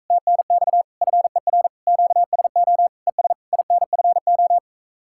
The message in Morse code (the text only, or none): TNX FER QSO ES INFO